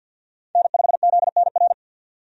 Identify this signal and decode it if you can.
Morse code, heard as a keyed tone, about 47 wpm, 700 Hz. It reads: N5ZNR
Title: USA Call Signs - 5 Characters - 50wpm